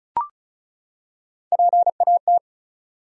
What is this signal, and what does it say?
Morse code, heard as a keyed tone, about 34 wpm, 700 Hz. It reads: PAT
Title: USA Common Names - 35wpm